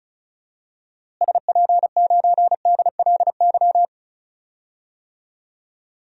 Morse code, heard as a keyed tone, about 36 wpm, 700 Hz. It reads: SP9DLY
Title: DXCC-100 Call Signs - 6 Characters - Repeated 3x - 35wpm